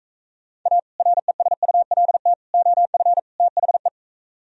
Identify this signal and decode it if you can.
Morse code, heard as a keyed tone, about 45 wpm, 700 Hz. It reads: A RESULT OF THE